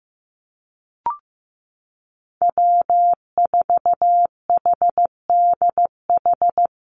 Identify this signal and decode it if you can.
Morse code, heard as a keyed tone, about 15 wpm, 700 Hz. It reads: W4HDH
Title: USA Call Signs Repeated 2x 15wpm